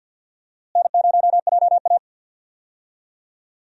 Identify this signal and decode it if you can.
Morse code, heard as a keyed tone, about 52 wpm, 700 Hz. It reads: N0JA
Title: USA Call Signs - 4 Characters - Repeated 2x - 50wpm